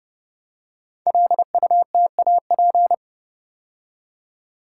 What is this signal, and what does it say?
Morse code, heard as a keyed tone, about 30 wpm, 700 Hz. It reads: LUTAP
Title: Code Groups - 5 Letters - 30wpm